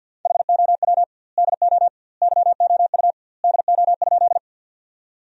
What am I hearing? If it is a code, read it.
Morse code, heard as a keyed tone, about 50 wpm, 700 Hz. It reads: HOW DO YOU DO?
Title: 2x Word-Spacing - Sentences from Top 100 Words - 50wpm